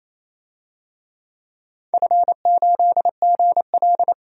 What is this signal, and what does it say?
Morse code, heard as a keyed tone, about 28 wpm, 700 Hz. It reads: F8GL
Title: DXCC-100 Call Signs - 4 Characters - 28wpm